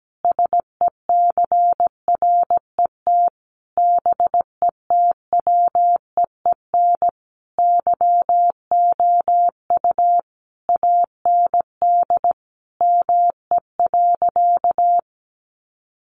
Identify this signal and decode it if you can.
Morse code, heard as a keyed tone, about 17 wpm, 700 Hz. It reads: SECRET BETWEEN YOU AND ME.
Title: Sentences from Top 1200 Words 17wpm